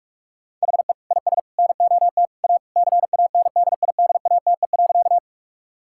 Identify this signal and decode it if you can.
Morse code, heard as a keyed tone, about 46 wpm, 700 Hz. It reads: HE IS NOT A CANDIDATE.
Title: Sentences from Top 1600 Words 45wpm